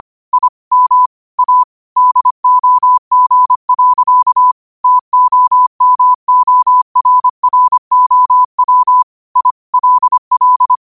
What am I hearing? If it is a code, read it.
Morse code, heard as a keyed tone, about 25 wpm, 1000 Hz. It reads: I M A DOG. TOMORROW I LL
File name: 25wpm